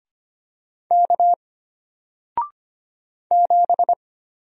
Morse code, heard as a keyed tone, about 25 wpm, 700 Hz. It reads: K 7
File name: Single-Letter-Number-25wpm